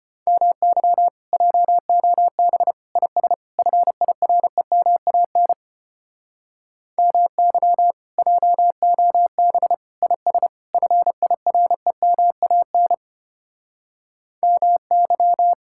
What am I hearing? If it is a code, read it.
Morse code, heard as a keyed tone, about 33 wpm, 700 Hz. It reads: MY JOB IS FIREMAN MY JOB IS FIREMAN MY